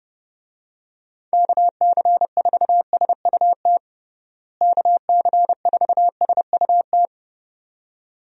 Morse code, heard as a keyed tone, about 32 wpm, 700 Hz. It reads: KC4SUT KC4SUT
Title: USA Call Signs - 6 Characters - Repeated 2x - 30wpm